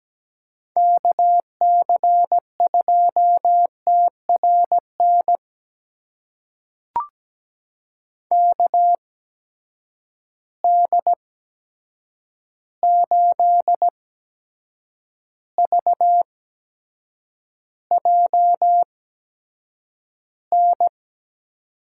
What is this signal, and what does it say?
Morse code, heard as a keyed tone, about 17 wpm, spaced as for 7 wpm, 700 Hz. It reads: KC2TRN KD8VJN
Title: USA Call Signs - 6 Characters - 8x Character Spacing - 17wpm